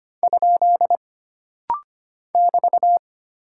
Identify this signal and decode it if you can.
Morse code, heard as a keyed tone, about 25 wpm, 700 Hz. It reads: ? <BT>
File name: Single Punctuation-Prosign - Rapid-Fire - 25wpm